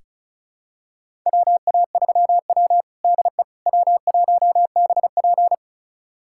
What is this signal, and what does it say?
Morse code, heard as a keyed tone, about 34 wpm, 700 Hz. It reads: WA3W DE W1BP
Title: USA Callsign DE Callsign - 4 Characters - 35wpm